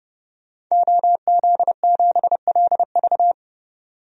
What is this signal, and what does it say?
Morse code, heard as a keyed tone, about 30 wpm, 700 Hz. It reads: OZ7LV
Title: DXCC-100 Call Signs - 5 Characters - 30wpm